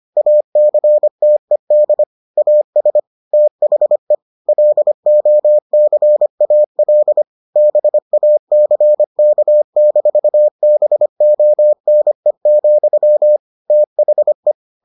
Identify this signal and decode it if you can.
Morse code, heard as a keyed tone, about 25 wpm, 600 Hz. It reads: ACTED AS THE LOCAL BACK-BONE, THE